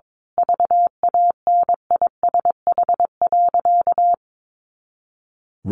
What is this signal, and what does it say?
Morse code, heard as a keyed tone, about 22 wpm, 700 Hz. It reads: VANISH.
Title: A Sound Of Thunder 22wpm